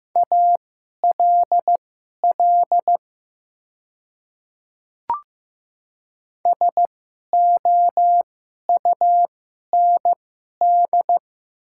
Morse code, heard as a keyed tone, about 15 wpm, spaced as for 11 wpm, 700 Hz. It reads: ALL SOUND
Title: 2x Character Spacing - Top 100 Words - 15wpm